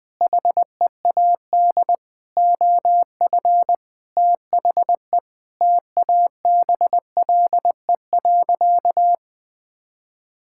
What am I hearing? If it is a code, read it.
Morse code, heard as a keyed tone, about 21 wpm, 700 Hz. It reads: HEAD OF THE TABLE.
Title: Sentences with 5-Word Phrases - 20wpm